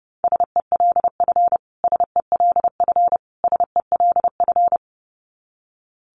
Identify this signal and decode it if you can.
Morse code, heard as a keyed tone, about 28 wpm, 700 Hz. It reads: SELF SELF SELF
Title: Top 500 Words Repeated 30wpm